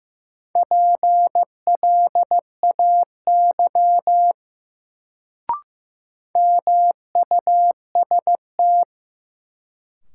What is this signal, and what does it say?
Morse code, heard as a keyed tone, about 15 wpm, 700 Hz. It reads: PLAY MUST
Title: Sentences from Top 200 Words - New Word Drill - 15wpm